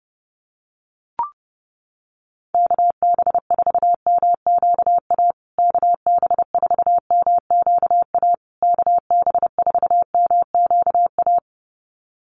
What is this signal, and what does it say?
Morse code, heard as a keyed tone, about 30 wpm, 700 Hz. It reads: KB4MQA KB4MQA KB4MQA
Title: USA Call Signs Repeated 3x 30wpm